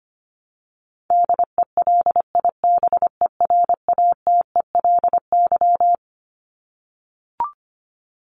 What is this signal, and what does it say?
Morse code, heard as a keyed tone, about 25 wpm, 700 Hz. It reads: DELIBERATELY
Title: Top 1751-2000 Words 25wpm